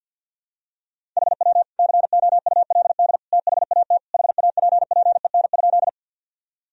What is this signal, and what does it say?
Morse code, heard as a keyed tone, about 52 wpm, 700 Hz. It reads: HW COULD THAT HAPPEN?
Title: Sentences from Top 400 Words and Abbreviations 50wpm